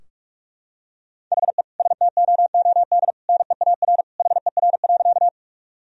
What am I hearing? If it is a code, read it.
Morse code, heard as a keyed tone, about 46 wpm, 700 Hz. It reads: HE STOOD NEAR HER.
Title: Sentences from Top 1000 Words - Review Part 2 - 45wpm